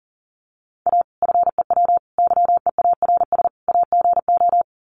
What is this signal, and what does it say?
Morse code, heard as a keyed tone, about 40 wpm, 700 Hz. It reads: A FEW YEARS AGO
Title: Sets of 4 Words 40wpm